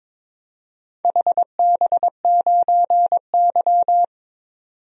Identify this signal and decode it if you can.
Morse code, heard as a keyed tone, about 22 wpm, 700 Hz. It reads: HB9Y
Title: DXCC-100 Call Signs - 4 Characters - 22wpm